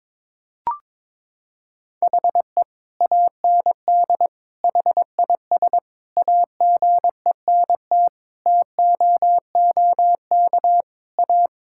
Morse code, heard as a keyed tone, about 21 wpm, 700 Hz. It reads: HE AND HIS AGENT TOOK A
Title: Sentences from Top 1600 Words 22wpm